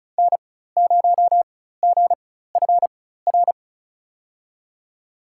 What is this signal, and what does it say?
Morse code, heard as a keyed tone, about 36 wpm, spaced as for 16 wpm, 700 Hz. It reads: N0GFR
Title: USA Call Signs - 5 Characters - 4x Character Spacing - 35wpm